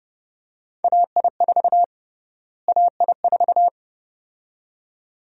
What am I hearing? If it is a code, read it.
Morse code, heard as a keyed tone, about 30 wpm, 700 Hz. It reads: AI4 AI4
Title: USA Call Sign Prefixes - Repeated 2x - 30wpm